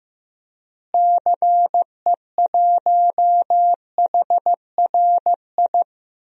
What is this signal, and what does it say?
Morse code, heard as a keyed tone, about 15 wpm, 700 Hz. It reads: CE1HRI
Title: DXCC-100 Call Signs - 6 Characters - Repeated 2x - 15wpm